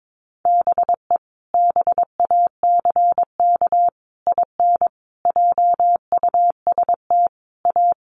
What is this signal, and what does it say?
Morse code, heard as a keyed tone, about 22 wpm, 700 Hz. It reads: BE BACK IN JUST A
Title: Sets of 5 Words Encore 22wpm